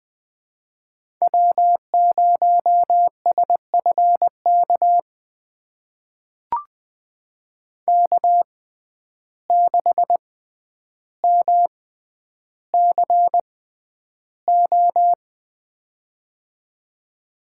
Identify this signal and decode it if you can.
Morse code, heard as a keyed tone, about 20 wpm, spaced as for 10 wpm, 700 Hz. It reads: W0SFK K6MCO
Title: USA Call Signs - 5 Characters - 6x Character Spacing - 20wpm